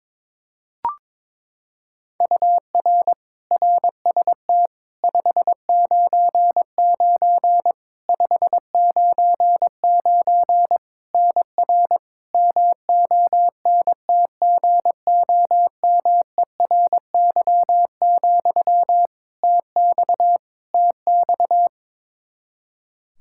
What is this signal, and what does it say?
Morse code, heard as a keyed tone, about 22 wpm, 700 Hz. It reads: UR RST 599 599 NR MONTGOMERY, TX TX
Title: Practice QSO - Exchange 1 - Beginning - 22wpm